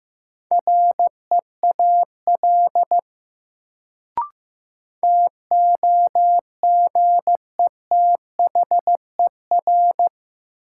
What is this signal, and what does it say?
Morse code, heard as a keyed tone, about 15 wpm, 700 Hz. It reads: REAL TOGETHER
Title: Sentences from Top 300 Words - New Word Drill - 15wpm